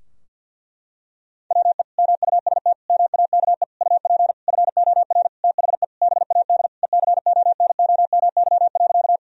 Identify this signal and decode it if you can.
Morse code, heard as a keyed tone, about 52 wpm, 700 Hz. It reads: WE MUST MAKE UP FOR THE BAD ECONOMY.
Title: Sentences from Top 1000 Words - Review Part 3 - 50wpm